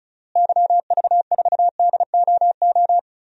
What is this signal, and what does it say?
Morse code, heard as a keyed tone, about 36 wpm, 700 Hz. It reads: YV4DOO
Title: DXCC-100 Call Signs - 6 Characters - Repeated 3x - 35wpm